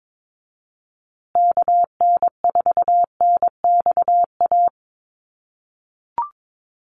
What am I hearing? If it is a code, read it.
Morse code, heard as a keyed tone, about 22 wpm, 700 Hz. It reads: KN4NXA
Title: USA Call Signs Repeated 3x 22wpm